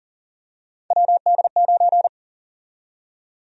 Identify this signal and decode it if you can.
Morse code, heard as a keyed tone, about 40 wpm, 700 Hz. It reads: WD9
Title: USA Call Sign Prefixes - 40wpm